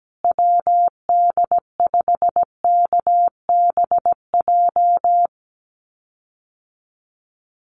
Morse code, heard as a keyed tone, about 17 wpm, 700 Hz. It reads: WD5KBJ
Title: USA Call Signs Repeated 2x 17wpm